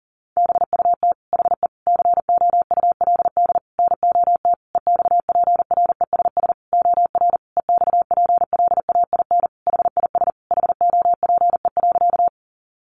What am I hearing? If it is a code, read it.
Morse code, heard as a keyed tone, about 40 wpm, 700 Hz. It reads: BUT HE COULD NOT EXPRESS OR EXPLAIN HIS HOPE.
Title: Sentences from Top 1100 Words 40wpm